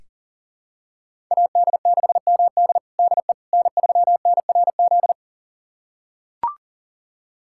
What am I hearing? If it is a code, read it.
Morse code, heard as a keyed tone, about 40 wpm, 700 Hz. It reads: AD6MD DE N3NRZ
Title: USA Callsign DE Callsign - 5 Characters - 40wpm